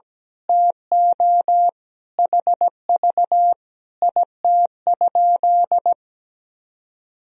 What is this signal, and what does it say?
Morse code, heard as a keyed tone, about 17 wpm, 700 Hz. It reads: TO HV IT?
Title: Sentences from Top 100 Words and Abbreviations 17wpm